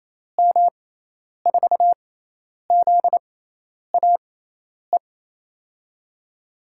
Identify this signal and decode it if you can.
Morse code, heard as a keyed tone, about 28 wpm, spaced as for 10 wpm, 700 Hz. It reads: M4ZAE
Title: USA Call Signs - 6 Characters - 6x Character Spacing - 28wpm